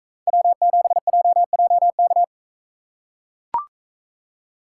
Morse code, heard as a keyed tone, about 40 wpm, 700 Hz. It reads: W7JJK